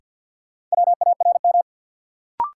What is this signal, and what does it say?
Morse code, heard as a keyed tone, about 53 wpm, 700 Hz. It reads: WARM
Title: Sentences from Top 500 Words - New Word Drill - 50wpm